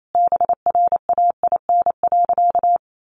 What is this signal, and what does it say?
Morse code, heard as a keyed tone, about 28 wpm, 700 Hz. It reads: BRAIN.
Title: Sentences from Top 1100 Words 28wpm